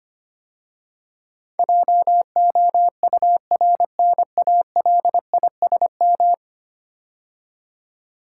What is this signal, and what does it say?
Morse code, heard as a keyed tone, about 25 wpm, 700 Hz. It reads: JOURNALISM
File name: 10 Letter Words - First 3 Letters Given - 25wpm